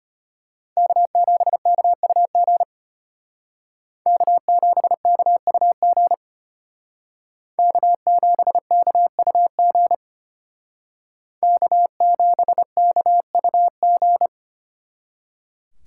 Morse code, heard as a keyed tone, about 32 wpm, 700 Hz. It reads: K7KUG K7KUG K7KUG K7KUG